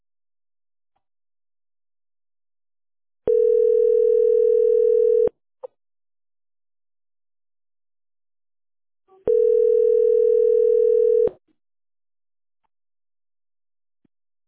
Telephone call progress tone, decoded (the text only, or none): ringback tone